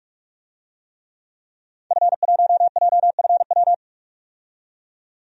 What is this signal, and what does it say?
Morse code, heard as a keyed tone, about 46 wpm, 700 Hz. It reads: F1JFW
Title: DXCC-100 Call Signs - 5 Characters - 45wpm